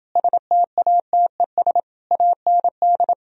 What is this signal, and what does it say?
Morse code, heard as a keyed tone, about 30 wpm, 700 Hz. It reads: STATES AND